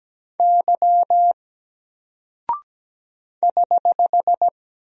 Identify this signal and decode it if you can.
Morse code, heard as a keyed tone, about 17 wpm, 700 Hz. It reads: Y <HH>
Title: Single Character (Letter:Number:Punctuation:Prosign) 17wpm